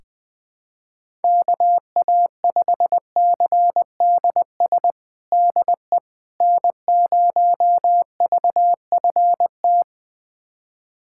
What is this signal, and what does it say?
Morse code, heard as a keyed tone, about 20 wpm, 700 Hz. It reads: KA5CDS DE N0VFT
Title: USA Callsign DE Callsign 20wpm